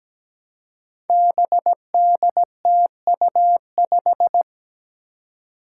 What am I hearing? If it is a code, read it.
Morse code, heard as a keyed tone, about 17 wpm, 700 Hz. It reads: BDTU5
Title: Five Challenging Letter-Number Combinations 17wpm